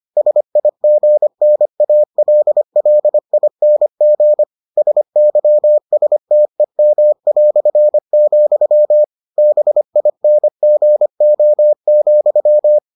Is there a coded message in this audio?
Yes: Morse code from 0.2 to 12.8 s, heard as a keyed tone, about 25 wpm, 600 Hz. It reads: SIGNALLING SYSTEM", BINGO,